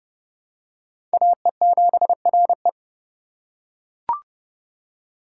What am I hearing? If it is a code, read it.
Morse code, heard as a keyed tone, about 27 wpm, 700 Hz. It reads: AE7RE